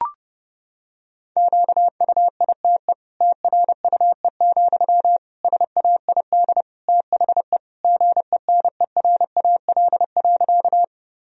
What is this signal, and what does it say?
Morse code, heard as a keyed tone, about 30 wpm, 700 Hz. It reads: QUITE TRUE, SAID THE GENERAL.
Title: Sentences from Top 1000 Words - Review Part 3 - 30wpm